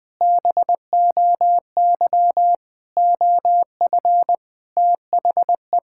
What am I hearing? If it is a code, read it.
Morse code, heard as a keyed tone, about 20 wpm, 700 Hz. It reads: BOY OF THE
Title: Sentences from Top 1000 Words - Review Part 5 - 20wpm